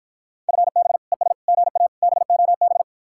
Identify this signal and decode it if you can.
Morse code, heard as a keyed tone, about 60 wpm, 700 Hz. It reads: FB ES GA BOB